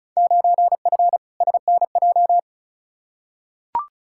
Morse code, heard as a keyed tone, about 34 wpm, 700 Hz. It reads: 9F SNJ
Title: Field Day Exchanges Encore - 35wpm